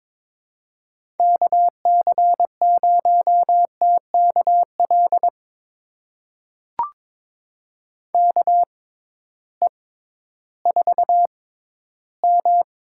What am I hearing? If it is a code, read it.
Morse code, heard as a keyed tone, about 21 wpm, spaced as for 13 wpm, 700 Hz. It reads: KC0TKL KE4M
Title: USA Call Signs - 6 Characters - 6x Character Spacing - 22wpm